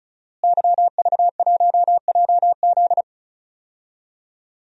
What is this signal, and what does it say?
Morse code, heard as a keyed tone, about 36 wpm, 700 Hz. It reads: YV1JZ
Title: DXCC-100 Call Signs - 5 Characters - Repeated 3x - 35wpm